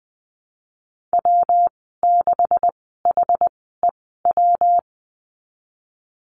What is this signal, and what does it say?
Morse code, heard as a keyed tone, about 20 wpm, spaced as for 15 wpm, 700 Hz. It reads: W6HEW
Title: 140 Lesson 23g - Letter f - Call Signs 20wpm